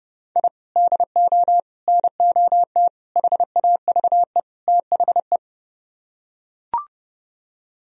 Morse code, heard as a keyed tone, about 30 wpm, 700 Hz. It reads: I DO NOT HAVE THE